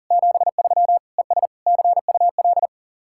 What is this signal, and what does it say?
Morse code, heard as a keyed tone, about 49 wpm, 700 Hz. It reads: 73 ES CUL
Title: Speed-Racing Common QSO Phrases Encore - 40wpm